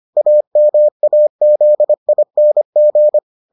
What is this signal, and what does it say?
Morse code, heard as a keyed tone, about 25 wpm, 600 Hz. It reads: AMAZING